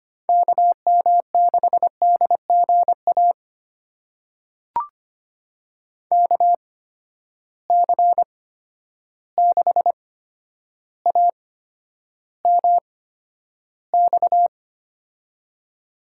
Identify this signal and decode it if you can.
Morse code, heard as a keyed tone, about 25 wpm, spaced as for 11 wpm, 700 Hz. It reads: KM6DGA KC6AMX
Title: USA Call Signs - 6 Characters - 8x Character Spacing - 25wpm